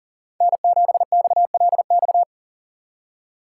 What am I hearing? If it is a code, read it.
Morse code, heard as a keyed tone, about 38 wpm, 700 Hz. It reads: N7XLX
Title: USA Call Signs - 5 Characters - 3x Character Spacing - 40wpm